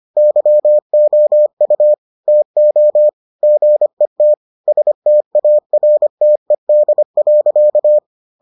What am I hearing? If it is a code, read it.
Morse code, heard as a keyed tone, about 25 wpm, 600 Hz. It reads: YOU TO GET STARTED.